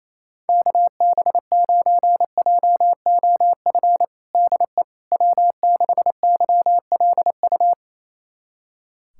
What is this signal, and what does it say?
Morse code, heard as a keyed tone, about 28 wpm, 700 Hz. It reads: KB9JOF DE W6YLU
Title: USA Callsign DE Callsign 28wpm